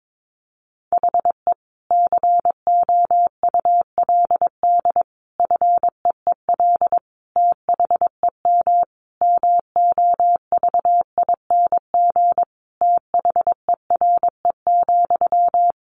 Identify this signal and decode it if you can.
Morse code, heard as a keyed tone, about 22 wpm, 700 Hz. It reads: HE COULD FEEL THEM MOVING THERE,
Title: A Sound Of Thunder 22wpm